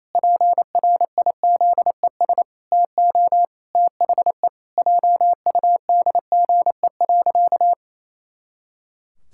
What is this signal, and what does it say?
Morse code, heard as a keyed tone, about 28 wpm, 700 Hz. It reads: PRIZES TO THE JUDGE.
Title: Pangram Challenge v2 - 28wpm